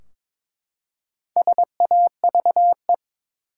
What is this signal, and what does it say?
Morse code, heard as a keyed tone, about 22 wpm, 700 Hz. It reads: SAVE
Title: Four Letter Words - 22wpm